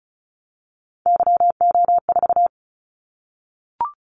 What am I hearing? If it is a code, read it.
Morse code, heard as a keyed tone, about 36 wpm, 700 Hz. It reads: YO4
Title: Callsign Prefixes for Top 25 DXCC - 35wpm